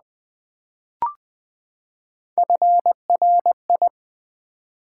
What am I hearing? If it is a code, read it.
Morse code, heard as a keyed tone, about 20 wpm, 700 Hz. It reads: FRI
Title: Build A Word - 3 Letters At A Time - 6-Letter Words - 20wpm